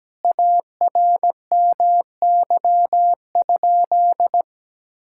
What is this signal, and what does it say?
Morse code, heard as a keyed tone, about 17 wpm, 700 Hz. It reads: ARMY?
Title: Sentences from Top 1000 Words - Review Part 5 - 17wpm